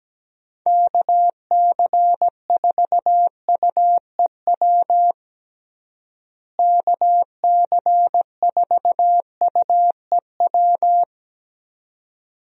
Morse code, heard as a keyed tone, about 17 wpm, 700 Hz. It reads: KC4UEW KC4UEW
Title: USA Call Signs - 6 Characters - Repeated 2x - 17wpm